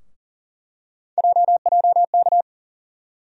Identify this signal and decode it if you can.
Morse code, heard as a keyed tone, about 40 wpm, 700 Hz. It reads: JJK
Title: USA Call Sign Postfixes - 3 Letters 40wpm